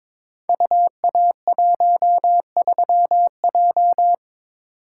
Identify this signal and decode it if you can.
Morse code, heard as a keyed tone, about 23 wpm, 700 Hz. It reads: UA13J
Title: Code Groups - 5 Letter:Numbers - 22wpm